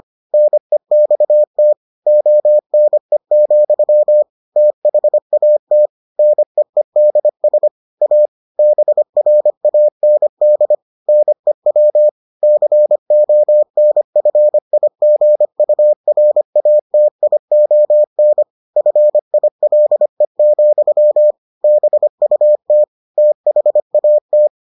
Morse code, heard as a keyed tone, about 25 wpm, 600 Hz. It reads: NEXT ONE, THAT NEEDS A BRAND NEW CONFIGURATION FILE, BUT THAT